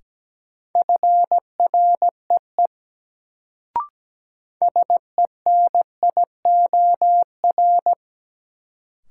Morse code, heard as a keyed tone, about 17 wpm, 700 Hz. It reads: FREE SENIOR
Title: Sets of 5 Words Encore - New Word Drill - 17wpm